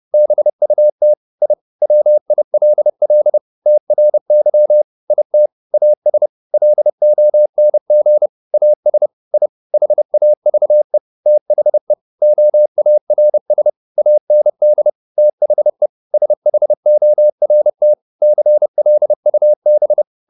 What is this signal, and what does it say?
Morse code, heard as a keyed tone, about 30 wpm, 600 Hz. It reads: BUT I WILL TRY IT AS LONG AS I HAVE THE OARS AND THE SHORT CLUB